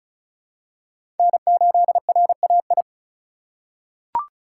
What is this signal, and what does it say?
Morse code, heard as a keyed tone, about 34 wpm, 700 Hz. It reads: N8RAI
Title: USA Call Signs - 6x Character Spacing - 35wpm